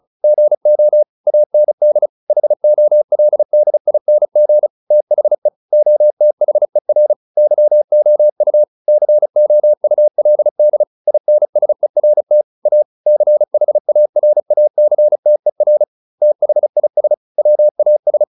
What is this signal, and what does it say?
Morse code, heard as a keyed tone, about 35 wpm, 600 Hz. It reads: GO AND HOLDING THE OTHER YOU COULD INSERT A CHARACTER THIS WAS